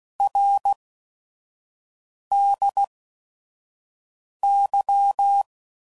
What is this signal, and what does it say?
Morse code, heard as a keyed tone, about 16 wpm, spaced as for 5 wpm, 800 Hz. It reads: RDY
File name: Mixed / Abreviations / Abreviations